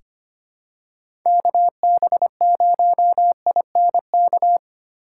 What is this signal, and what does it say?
Morse code, heard as a keyed tone, about 25 wpm, 700 Hz. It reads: KB0INK
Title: USA Call Signs - 6 Characters - 2x Character Spacing - 25wpm